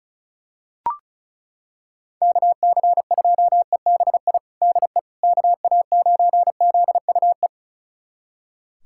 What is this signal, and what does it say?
Morse code, heard as a keyed tone, about 36 wpm, 700 Hz. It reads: KC2EBI DE KA9ZUE
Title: USA Callsign DE Callsign 35wpm